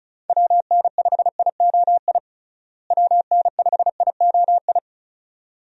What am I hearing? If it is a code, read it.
Morse code, heard as a keyed tone, about 36 wpm, 700 Hz. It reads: WN5IOI WN5IOI
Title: USA Call Signs - 6 Characters - Repeated 2x - 35wpm